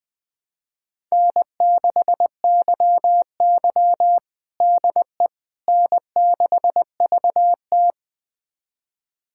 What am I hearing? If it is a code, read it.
Morse code, heard as a keyed tone, about 20 wpm, 700 Hz. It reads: N6YY DE N6VT
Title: USA Callsign DE Callsign - 4 Characters - 20wpm